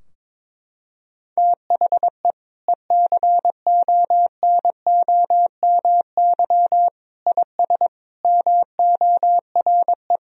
Morse code, heard as a keyed tone, about 23 wpm, 700 Hz. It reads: THE ECONOMY IS MORE
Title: Sentences from Top 1000 Words - Review Part 3 - 22wpm